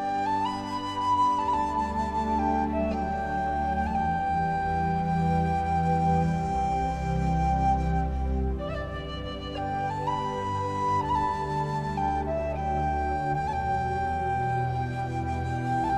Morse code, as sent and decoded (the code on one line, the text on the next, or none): none